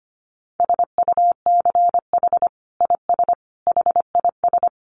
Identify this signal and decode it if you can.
Morse code, heard as a keyed tone, about 25 wpm, 700 Hz. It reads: SUCH IS HIS
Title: Sentences from Top 900 Words 25wpm